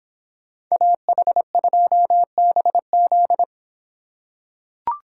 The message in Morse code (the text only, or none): AH2BZ